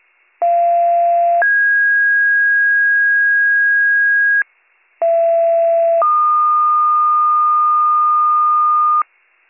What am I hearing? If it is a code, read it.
Two-tone page, heard as a one-second tone, then a three-second tone, about 690 then 1650 Hz, about 670 then 1220 Hz, with noise. It is heard from 0.4 to 9.0 s.